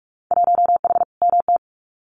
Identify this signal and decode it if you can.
Morse code, heard as a keyed tone, about 46 wpm, 700 Hz. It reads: 1H MT
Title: Winter Field Day Exchanges Encore 45wpm